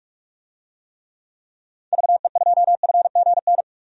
Morse code, heard as a keyed tone, about 44 wpm, 700 Hz. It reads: VE2FGN
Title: DXCC-100 Call Signs - 6 Characters - 45wpm